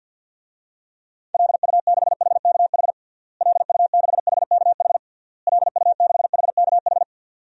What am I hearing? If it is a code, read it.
Morse code, heard as a keyed tone, about 53 wpm, 700 Hz. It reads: LU6HKH LU6HKH LU6HKH
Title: DXCC-100 Call Signs - 6 Characters - Repeated 3x - 50wpm